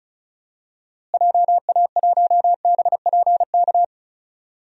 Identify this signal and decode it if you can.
Morse code, heard as a keyed tone, about 36 wpm, 700 Hz. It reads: JA1BPK
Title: DXCC-100 Call Signs - 6 Characters - Repeated 3x - 35wpm